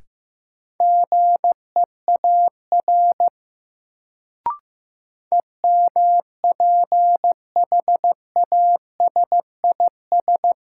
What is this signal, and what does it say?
Morse code, heard as a keyed tone, about 16 wpm, 700 Hz. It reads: GEAR EMPHASIS
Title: Sentences from Top 1400 Words - New Word Drill Alt - 15wpm